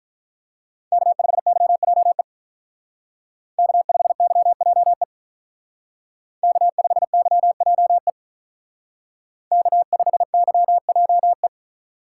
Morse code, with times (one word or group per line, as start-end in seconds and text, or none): 0.9-2.2 K5YJE
3.6-5.0 K5YJE
6.4-8.1 K5YJE
9.5-11.5 K5YJE